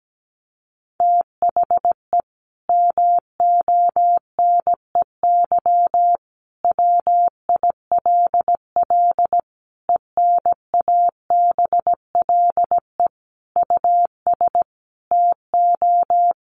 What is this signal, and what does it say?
Morse code, heard as a keyed tone, about 17 wpm, 700 Hz. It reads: THE MONEY WILL ENABLE US TO
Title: Sentences from Top 1500 Words 17wpm